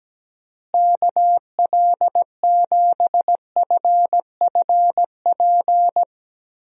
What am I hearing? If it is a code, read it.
Morse code, heard as a keyed tone, about 17 wpm, 700 Hz. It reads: KL7FFP
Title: USA Call Signs - 6 Characters - Repeated 3x - 17wpm